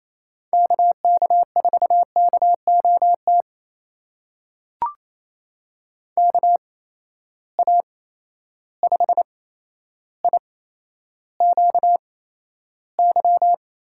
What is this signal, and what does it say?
Morse code, heard as a keyed tone, about 28 wpm, spaced as for 12 wpm, 700 Hz. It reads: KK4KOT KA5IQY
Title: USA Call Signs - 6 Characters - 8x Character Spacing - 28wpm